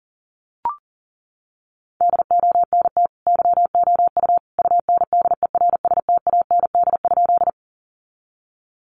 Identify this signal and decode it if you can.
Morse code, heard as a keyed tone, about 40 wpm, 700 Hz. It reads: DONT YOU UNDERSTAND?
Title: Sentences from Top 600 Words 40wpm